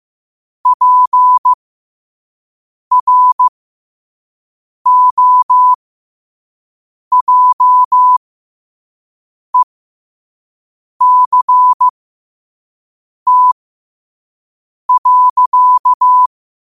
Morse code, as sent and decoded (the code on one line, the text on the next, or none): .--. .-. --- .--- . -.-. - .-.-.-
PROJECT.